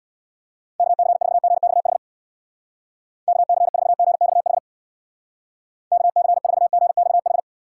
Morse code, heard as a keyed tone, about 61 wpm, 700 Hz. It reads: KC4GXH KC4GXH KC4GXH
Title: Speed Racing - USA 2x3 Call Signs 50wpm